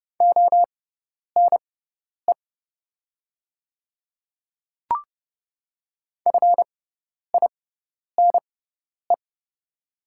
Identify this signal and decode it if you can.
Morse code, heard as a keyed tone, about 31 wpm, spaced as for 10 wpm, 700 Hz. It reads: ONE FINE
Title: 6x Character Spacing - Top 500 Words - 30wpm